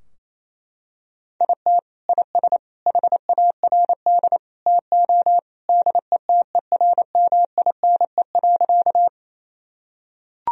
Morse code, heard as a keyed tone, about 28 wpm, 700 Hz. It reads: IT IS HARD TO DETERMINE.
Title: Sentences from Top 1000 Words - Review Part 4 - 28wpm